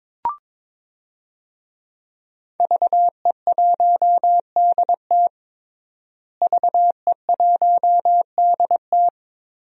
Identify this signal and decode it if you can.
Morse code, heard as a keyed tone, about 22 wpm, 700 Hz. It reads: VE1DT VE1DT
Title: DXCC-100 Call Signs - 5 Characters - Repeated 3x - 22wpm